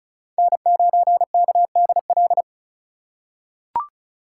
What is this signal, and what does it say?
Morse code, heard as a keyed tone, about 34 wpm, 700 Hz. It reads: N9KDL
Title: USA Call Signs - 5 Characters - 2x Character Spacing - 35wpm